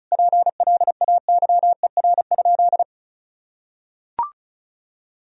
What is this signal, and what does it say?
Morse code, heard as a keyed tone, about 36 wpm, 700 Hz. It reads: PLAYER?
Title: Sentences from Top 1000 Words - Review Part 4 - 35wpm